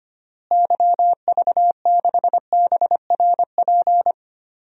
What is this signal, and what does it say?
Morse code, heard as a keyed tone, about 25 wpm, 700 Hz. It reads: YV6BRP
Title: DXCC-100 Call Signs - 6 Characters - Repeated 2x - 25wpm